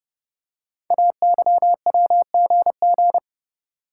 Morse code, heard as a keyed tone, about 30 wpm, 700 Hz. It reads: AYWGG
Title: Code Groups - 5 Letters - 30wpm